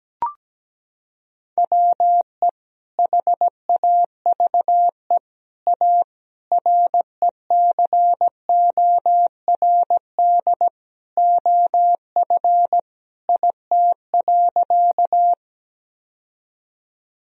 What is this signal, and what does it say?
Morse code, heard as a keyed tone, about 17 wpm, 700 Hz. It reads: WE HAVE A RECORD OF IT.
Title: Sentences from Top 1000 Words - Review Part 3 - 17wpm